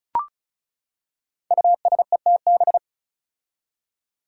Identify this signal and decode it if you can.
Morse code, heard as a keyed tone, about 36 wpm, 700 Hz. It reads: USETB
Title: Five Challenging Letter-Number Combinations 35wpm